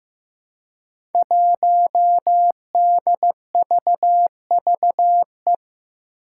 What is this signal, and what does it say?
Morse code, heard as a keyed tone, about 15 wpm, 700 Hz. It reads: 1DVVE
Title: Five Challenging Letter-Number Combinations 15wpm